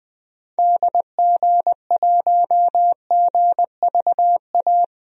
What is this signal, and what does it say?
Morse code, heard as a keyed tone, about 21 wpm, 700 Hz. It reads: DG1GVA
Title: DXCC-100 Call Signs - 6 Characters - Repeated 2x - 20wpm